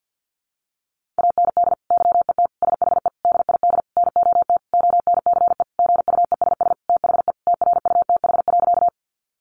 Morse code, heard as a keyed tone, about 50 wpm, 700 Hz. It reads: AND YET SHE DID NOT ONCE GUESS THE TRUTH.
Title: Sentences from Top 1500 Words 50wpm